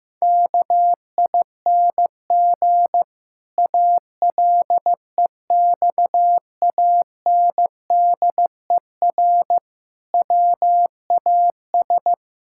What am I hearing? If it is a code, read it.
Morse code, heard as a keyed tone, about 14 wpm, 700 Hz. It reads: KING ALEXANDER WAS
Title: Pangram Challenge v2 - 15wpm